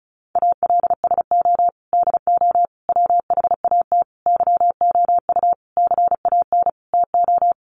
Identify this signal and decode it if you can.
Morse code, heard as a keyed tone, about 34 wpm, 700 Hz. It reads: ALSO DO WHAT YOU CAN TO
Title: Sentences from Top 1300 Words 35wpm